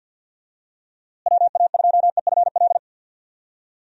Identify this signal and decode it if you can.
Morse code, heard as a keyed tone, about 47 wpm, 700 Hz. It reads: WA2EFL
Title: USA Call Signs - 6 Characters - Repeated 3x - 50wpm